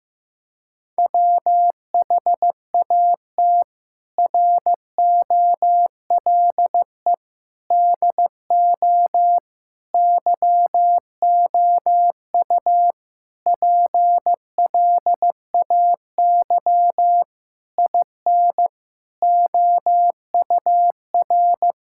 Morse code, heard as a keyed tone, about 15 wpm, 700 Hz. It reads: WHAT ROLE DO YOU PLAY IN OUR
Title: Sentences from Top 1000 Words - Review Part 6 - 15wpm